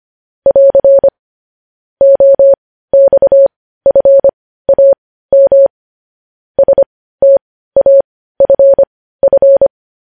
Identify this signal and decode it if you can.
Morse code, heard as a keyed tone, about 25 wpm, spaced as for 15 wpm, 550 Hz. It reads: <AR> OXFAM STAFF
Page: news-headlines-in-morse-code-at-15-wpm